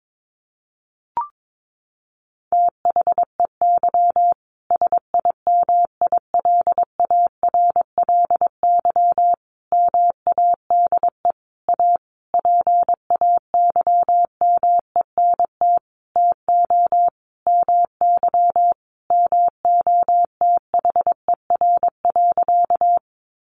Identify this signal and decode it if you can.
Morse code, heard as a keyed tone, about 22 wpm, 700 Hz. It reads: THEY SIMILARLY MADE A PAYMENT TO MY MOTHER.
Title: Sentences from Top 1400 Words 22wpm